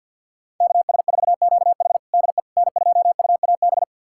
Morse code, heard as a keyed tone, about 47 wpm, 700 Hz. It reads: KS4QH DE N2UAB
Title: USA Callsign DE Callsign - 5 Characters - 50wpm